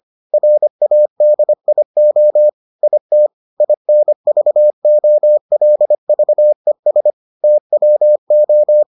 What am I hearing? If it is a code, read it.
Morse code, heard as a keyed tone, about 25 wpm, 600 Hz. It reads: RADIO IT INVOLVES TWO